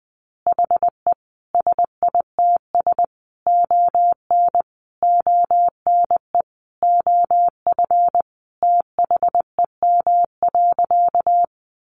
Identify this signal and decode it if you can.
Morse code, heard as a keyed tone, about 20 wpm, 700 Hz. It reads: HE SITS ON ONE OF THEM.
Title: Additional Easy Phrases 20wpm